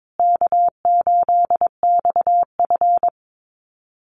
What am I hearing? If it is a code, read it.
Morse code, heard as a keyed tone, about 22 wpm, 700 Hz. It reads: K8XF
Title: Easy Call Signs Repeated 2x - 22wpm